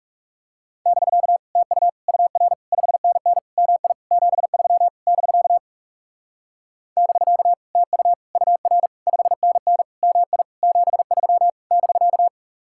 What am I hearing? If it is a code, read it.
Morse code, heard as a keyed tone, about 40 wpm, 700 Hz. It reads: <BK> TU UR 5NN MI 73 <BK> <BK> TU UR 5NN MI 73 <BK>